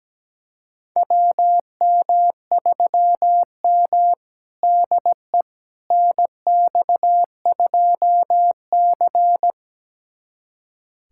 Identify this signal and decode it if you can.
Morse code, heard as a keyed tone, about 17 wpm, 700 Hz. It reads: WM3M DE NX2C
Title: USA Callsign DE Callsign - 4 Characters - 17wpm